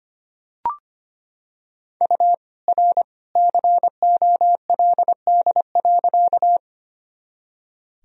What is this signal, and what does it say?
Morse code, heard as a keyed tone, about 25 wpm, 700 Hz. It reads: U R COLD.
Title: Sentences from Top 400 Words and Abbreviations 25wpm